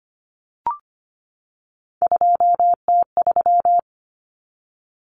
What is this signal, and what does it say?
Morse code, heard as a keyed tone, about 25 wpm, 700 Hz. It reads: 2T3
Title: Common Cut Numbers - 3-Digit 25wpm